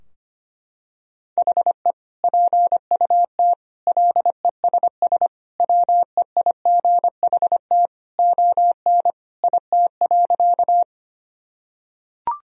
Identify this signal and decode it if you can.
Morse code, heard as a keyed tone, about 25 wpm, 700 Hz. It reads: HE PUT LESS WEIGHT ON IT.
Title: Sentences from Top 1000 Words - Review Part 3 - 25wpm